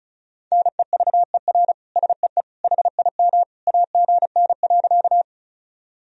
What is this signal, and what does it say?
Morse code, heard as a keyed tone, about 36 wpm, 700 Hz. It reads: NEVER SEE HIM AGN.
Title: Sentences from Top 300 Words and Abbreviations 35wpm